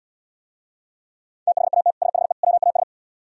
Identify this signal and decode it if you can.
Morse code, heard as a keyed tone, about 77 wpm, 700 Hz. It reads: THAT SHE WAS